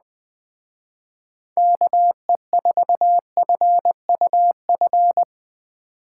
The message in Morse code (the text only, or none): KE4FUF